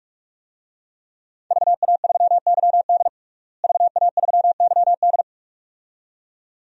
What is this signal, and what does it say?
Morse code, heard as a keyed tone, about 48 wpm, 700 Hz. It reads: VA3YD VA3YD
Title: DXCC-100 Call Signs - 5 Characters - Repeated 2x - 45wpm